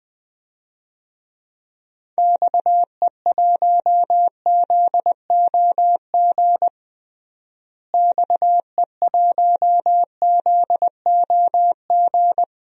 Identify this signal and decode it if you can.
Morse code, heard as a keyed tone, about 20 wpm, 700 Hz. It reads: XE1ZOG XE1ZOG
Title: DXCC-100 Call Signs - 6 Characters - Repeated 3x - 20wpm